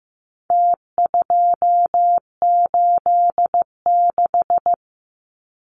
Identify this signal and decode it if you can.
Morse code, heard as a keyed tone, about 15 wpm, 700 Hz. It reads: T286
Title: Numbers - Contest Style - 4-Digit 15wpm